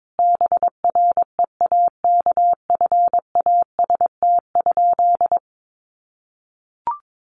Morse code, heard as a keyed tone, about 22 wpm, 700 Hz. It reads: BREAKFAST?
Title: Sentences from Top 1500 Words 22wpm